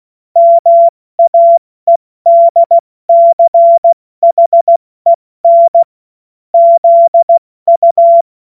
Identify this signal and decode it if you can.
Morse code, heard as a keyed tone, about 16 wpm, 680 Hz. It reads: MAEDCHEN ZU
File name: Morsekurs / Witze / Witz 24. Januar 2023 70 F80